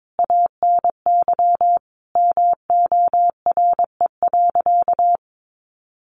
Morse code, heard as a keyed tone, about 22 wpm, 700 Hz. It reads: ANY MORE.
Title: Sentences from Top 1300 Words 22wpm